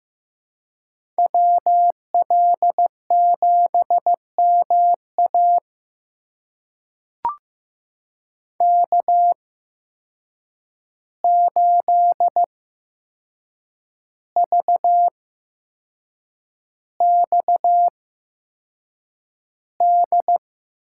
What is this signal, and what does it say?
Morse code, heard as a keyed tone, about 15 wpm, spaced as for 6 wpm, 700 Hz. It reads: WL7MA K8VXD
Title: USA Call Signs - 5 Characters - 8x Character Spacing - 15wpm